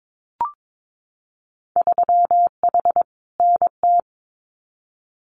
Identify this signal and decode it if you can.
Morse code, heard as a keyed tone, about 22 wpm, 700 Hz. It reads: 3H NT
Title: Winter Field Day Exchanges Encore 22wpm